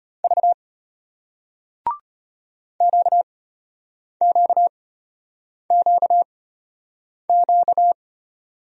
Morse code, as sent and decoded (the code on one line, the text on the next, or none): ...- / --.- --.- --.- --.-
V QQQQ